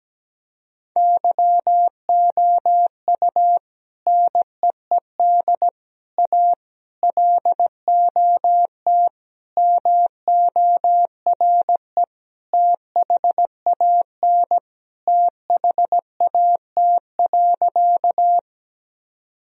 Morse code, heard as a keyed tone, about 17 wpm, 700 Hz. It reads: YOU NEED A LOT MORE THAN THAT.
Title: Sentences from Top 1000 Words - Review Part 3 - 17wpm